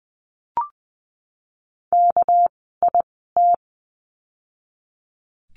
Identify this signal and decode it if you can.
Morse code, heard as a keyed tone, about 20 wpm, spaced as for 15 wpm, 700 Hz. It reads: KIT
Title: 199 Lesson 32f - Letter k - Words 20wpm